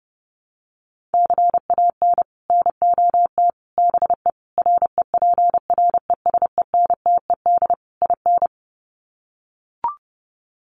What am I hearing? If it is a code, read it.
Morse code, heard as a keyed tone, about 30 wpm, 700 Hz. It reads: CAN NOT BE REPRESENTED IN
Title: Sets of 5 Words 30wpm